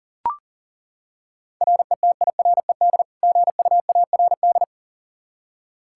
Morse code, heard as a keyed tone, about 40 wpm, 700 Hz. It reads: RETIRED GUARD
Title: K1USN SST QSO Practice 40wpm